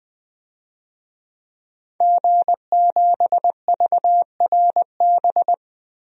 Morse code, heard as a keyed tone, about 20 wpm, 700 Hz. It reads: G7VRB
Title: DXCC-100 Call Signs - 5 Characters - Repeated 3x - 20wpm